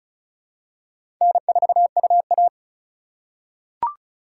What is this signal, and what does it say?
Morse code, heard as a keyed tone, about 34 wpm, 700 Hz. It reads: N4UA
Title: USA Call Signs - 4 Characters - 2x Character Spacing - 35wpm